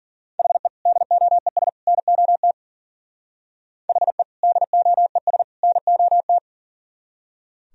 Morse code, heard as a keyed tone, about 52 wpm, 700 Hz. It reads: HE DOES NOT HE DOES NOT